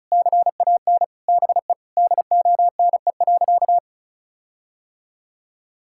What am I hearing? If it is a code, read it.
Morse code, heard as a keyed tone, about 34 wpm, 700 Hz. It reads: CAN BE DONE.
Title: Sentences from Top 1600 Words 35wpm